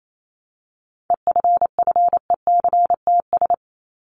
Morse code, heard as a keyed tone, about 28 wpm, 700 Hz. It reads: EFFECTS
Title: ES:S Suffix - Top 100 Words with Base Words 28wpm